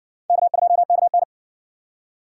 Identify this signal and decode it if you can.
Morse code, heard as a keyed tone, about 60 wpm, 700 Hz. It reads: K2KN